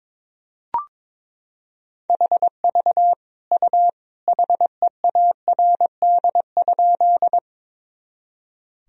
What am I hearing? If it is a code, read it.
Morse code, heard as a keyed tone, about 22 wpm, 700 Hz. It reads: HV U HEARD?
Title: Sentences from Top 400 Words and Abbreviations 22wpm